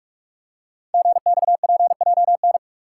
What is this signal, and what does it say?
Morse code, heard as a keyed tone, about 44 wpm, 700 Hz. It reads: GXPJN
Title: Code Groups - 5 Letter:Numbers - 45wpm